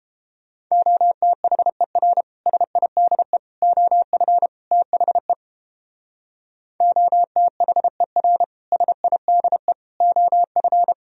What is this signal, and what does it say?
Morse code, heard as a keyed tone, about 30 wpm, 700 Hz. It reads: OTHER SIDE OF THE OTHER SIDE OF